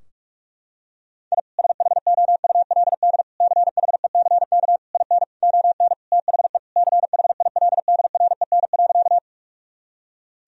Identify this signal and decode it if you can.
Morse code, heard as a keyed tone, about 46 wpm, 700 Hz. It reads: I SHOULD CHECK IN ON THE CHILDREN.
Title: Sentences from Top 1000 Words - Review Part 3 - 45wpm